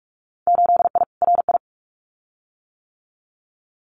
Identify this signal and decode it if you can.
Morse code, heard as a keyed tone, about 46 wpm, 700 Hz. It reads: 8I RI
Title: Winter Field Day Exchanges Encore 45wpm